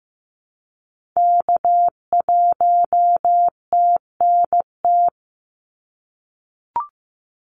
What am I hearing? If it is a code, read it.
Morse code, heard as a keyed tone, about 15 wpm, 700 Hz. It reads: K1TNT
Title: NA SOTA Activator Callsigns 15wpm